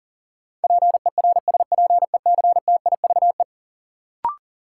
Morse code, heard as a keyed tone, about 40 wpm, 700 Hz. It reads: PERSPECTIVE
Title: Sentences from Top 1200 Words - New Word Drill - 40wpm